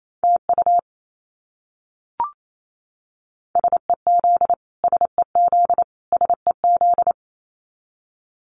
Morse code, heard as a keyed tone, about 28 wpm, 700 Hz. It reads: TU SEZ SEZ SEZ
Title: Abbreviations Repeated 3x - 28wpm